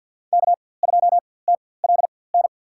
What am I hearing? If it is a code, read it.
Morse code, heard as a keyed tone, about 52 wpm, spaced as for 23 wpm, 700 Hz. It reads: K2TLN
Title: USA Call Signs - 5 Characters - 4x Character Spacing - 50wpm